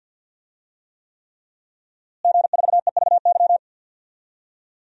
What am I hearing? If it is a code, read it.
Morse code, heard as a keyed tone, about 52 wpm, 700 Hz. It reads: G4EVY
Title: DXCC-100 Call Signs - 5 Characters - Repeated 3x - 50wpm